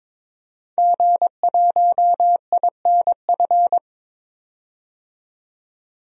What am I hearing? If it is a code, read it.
Morse code, heard as a keyed tone, about 22 wpm, 700 Hz. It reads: G1INF
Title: DXCC-100 Call Signs - 5 Characters - 22wpm